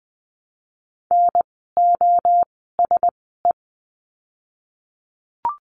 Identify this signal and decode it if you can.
Morse code, heard as a keyed tone, about 20 wpm, spaced as for 15 wpm, 700 Hz. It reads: NOSE
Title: 036 Lesson 7f - Letter s - Words 20wpm